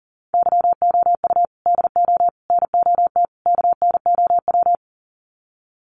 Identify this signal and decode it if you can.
Morse code, heard as a keyed tone, about 40 wpm, 700 Hz. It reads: YOU DO NOT KNOW
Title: Sets of 4 Words 40wpm